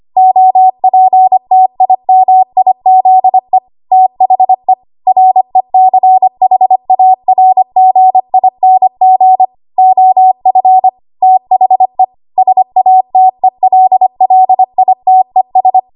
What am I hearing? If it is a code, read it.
Morse code, heard as a keyed tone, about 25 wpm, 750 Hz. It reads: OPTIMIZE THE RECHARGING OF THE SATELLITES
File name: Archive / 25 WPM / 25WPM